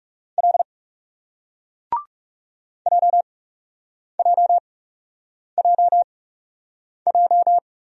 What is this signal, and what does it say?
Morse code, heard as a keyed tone, about 38 wpm, spaced as for 8 wpm, 700 Hz. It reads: L JJJJ